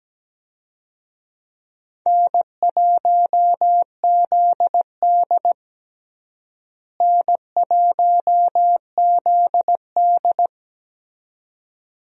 Morse code, heard as a keyed tone, about 17 wpm, 700 Hz. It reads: N1ZD N1ZD
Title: USA Call Signs - 4 Characters - Repeated 2x - 17wpm